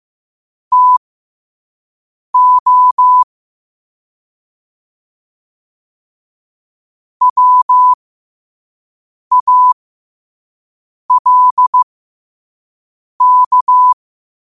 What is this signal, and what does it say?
Morse code, heard as a keyed tone, about 15 wpm, spaced as for 5 wpm, 1000 Hz. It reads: TO WALK
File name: qotdfiles / 5wpm